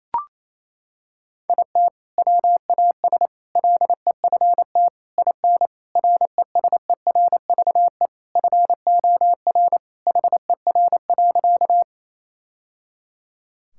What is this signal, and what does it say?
Morse code, heard as a keyed tone, about 28 wpm, 700 Hz. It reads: IT WAS LEFT IN RESERVE FOR HER.
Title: Sentences from Top 1000 Words - Review Part 6 - 28wpm